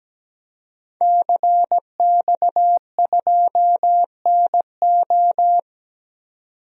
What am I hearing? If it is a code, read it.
Morse code, heard as a keyed tone, about 17 wpm, 700 Hz. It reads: CX2NO
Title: DXCC-100 Call Signs - 5 Characters - 17wpm